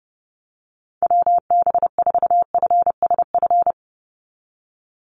Morse code, heard as a keyed tone, about 30 wpm, 700 Hz. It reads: WB4FSF
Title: Call Signs - 30wpm